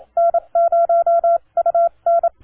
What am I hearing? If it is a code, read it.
Morse code, heard as a keyed tone, about 27 wpm, spaced as for 23 wpm, 1360 Hz, with noise. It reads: N0UN
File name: 3Y0J-17M-LP